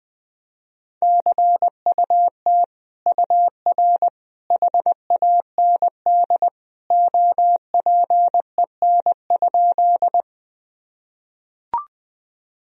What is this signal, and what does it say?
Morse code, heard as a keyed tone, about 20 wpm, 700 Hz. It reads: CUT UR HAND OPEN?
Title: Sentences from Top 300 Words and Abbreviations 20wpm